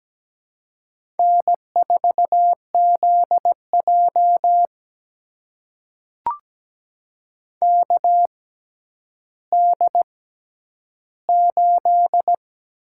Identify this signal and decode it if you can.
Morse code, heard as a keyed tone, about 17 wpm, spaced as for 10 wpm, 700 Hz. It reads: N4ZJ KD8
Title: USA Call Signs - 6x Character Spacing - 17wpm